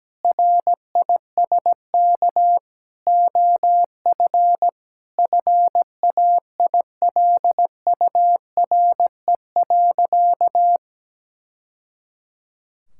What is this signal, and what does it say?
Morse code, heard as a keyed tone, about 17 wpm, 700 Hz. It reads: RISK OF FAILURE.
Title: Sentences from Top 1000 Words - Review Part 6 - 17wpm